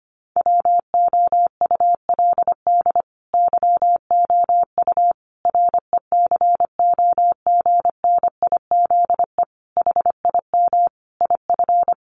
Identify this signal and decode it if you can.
Morse code, heard as a keyed tone, about 25 wpm, 700 Hz. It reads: WOULD YOU RECOGNIZE HIM IF
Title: Sentences from Top 1400 Words 25wpm